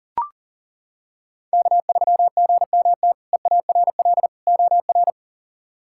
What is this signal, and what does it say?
Morse code, heard as a keyed tone, about 40 wpm, 690 Hz. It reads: K3GMT EARL OR
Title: K1USN SST QSO Practice - 40wpm